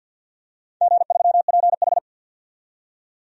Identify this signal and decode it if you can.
Morse code, heard as a keyed tone, about 45 wpm, 700 Hz. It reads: G3PH
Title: DXCC-100 Call Signs - 4 Characters - 50wpm